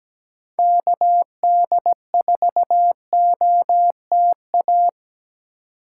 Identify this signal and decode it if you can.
Morse code, heard as a keyed tone, about 20 wpm, 700 Hz. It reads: KD4OTA
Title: Speed Racing - USA 2x3 Call Signs 17wpm